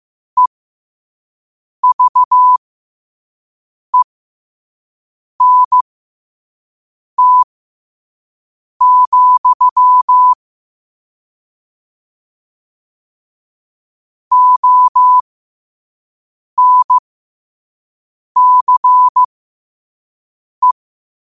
Morse code, heard as a keyed tone, about 15 wpm, spaced as for 5 wpm, 1000 Hz. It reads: EVENT, ONCE
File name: qotdfiles / 5wpm